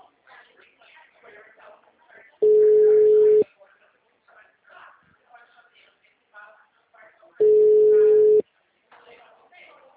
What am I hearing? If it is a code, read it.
Ringback tone (Europe, 425 Hz), with speech.
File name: out- -2006- - - .951